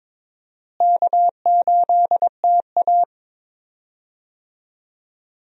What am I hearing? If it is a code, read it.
Morse code, heard as a keyed tone, about 22 wpm, 700 Hz. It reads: K8TA
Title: USA Call Signs - 4 Characters - Repeated 2x - 22wpm